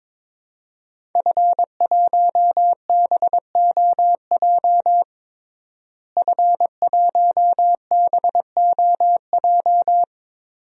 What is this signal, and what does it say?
Morse code, heard as a keyed tone, about 22 wpm, 700 Hz. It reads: F1BOJ F1BOJ
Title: DXCC-100 Call Signs - 5 Characters - Repeated 3x - 22wpm